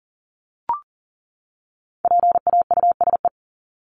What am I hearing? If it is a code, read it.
Morse code, heard as a keyed tone, about 40 wpm, 700 Hz. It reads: PAUSE
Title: Top 1501-1750 Words 40wpm